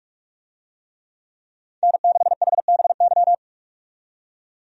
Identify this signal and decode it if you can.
Morse code, heard as a keyed tone, about 46 wpm, 700 Hz. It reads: N6HBY
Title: USA Call Signs - 5 Characters - 45wpm